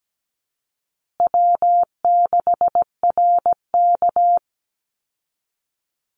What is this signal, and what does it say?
Morse code, heard as a keyed tone, about 17 wpm, 700 Hz. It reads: W6RK
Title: Easy Call Signs Repeated 2x - 17wpm